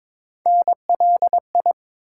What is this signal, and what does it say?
Morse code, heard as a keyed tone, about 22 wpm, 700 Hz. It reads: NLI
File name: ARRL Sections v2 - 22wpm